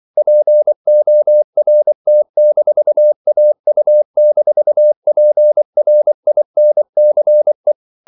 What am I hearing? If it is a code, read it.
Morse code, heard as a keyed tone, about 24 wpm, 600 Hz. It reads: PORT-AU-PRINCE
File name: CW / Practice / NationalCapitals24